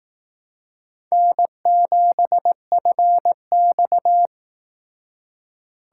Morse code, heard as a keyed tone, about 18 wpm, 700 Hz. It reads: N7FX
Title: Speed Racing - USA Call Signs 15wpm